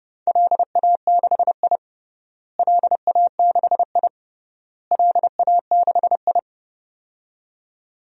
Morse code, heard as a keyed tone, about 32 wpm, 700 Hz. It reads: LA6I LA6I LA6I
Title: DXCC-100 Call Signs - 4 Characters - Repeated 3x - 30wpm